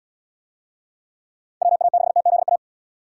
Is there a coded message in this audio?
Yes: Morse code from 1.6 to 2.6 s, heard as a keyed tone, about 69 wpm, 700 Hz. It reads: WA6TZA